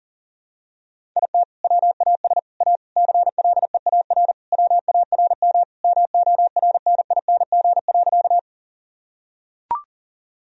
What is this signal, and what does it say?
Morse code, heard as a keyed tone, about 40 wpm, 700 Hz. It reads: IT WAS A CLEAR WARM MORNING.
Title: Sentences from Top 1000 Words - Review Part 2 - 40wpm